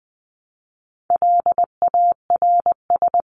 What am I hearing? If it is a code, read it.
Morse code, heard as a keyed tone, about 20 wpm, 700 Hz. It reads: LARS
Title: CWT Exchange 20wpm